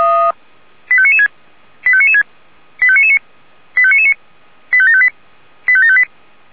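Five-tone selective call (ZVEI), heard as quick runs of five tones, tones of about 70 ms. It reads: 85996 85996 85999 85999 85758 85758
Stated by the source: ZVEI five-tone call 85730, received by SDR.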